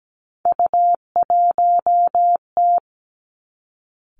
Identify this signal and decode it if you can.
Morse code, heard as a keyed tone, about 17 wpm, 700 Hz. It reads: U1T
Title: Three Challenging Letter-Number Combinations 17wpm